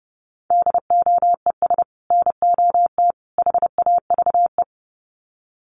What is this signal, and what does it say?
Morse code, heard as a keyed tone, about 30 wpm, 700 Hz. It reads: DOES NOT HAVE
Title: Sets of 3 Words 30wpm